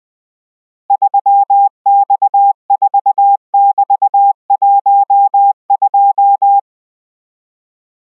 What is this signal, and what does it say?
Morse code, heard as a keyed tone, about 20 wpm, 800 Hz. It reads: 3X4<BT>12